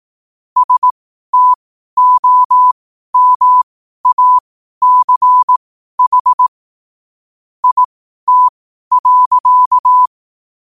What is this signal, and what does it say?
Morse code, heard as a keyed tone, about 18 wpm, spaced as for 12 wpm, 1000 Hz. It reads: STOMACH IT.
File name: qotdfiles / 12wpm